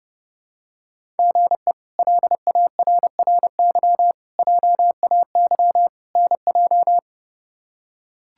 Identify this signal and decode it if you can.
Morse code, heard as a keyed tone, about 32 wpm, 700 Hz. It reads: GE LARRY JAY NJ
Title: K1USN SST QSO Practice - 30wpm